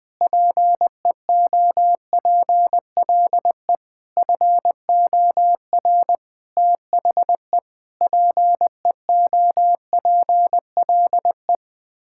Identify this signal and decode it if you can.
Morse code, heard as a keyed tone, about 20 wpm, 700 Hz. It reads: PEOPLE FOR THE PEOPLE
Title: Sets of 9 Words - 20wpm